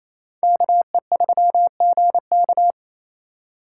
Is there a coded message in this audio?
Yes: Morse code from 0.4 to 2.7 s, heard as a keyed tone, about 35 wpm, 700 Hz. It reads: KE3GK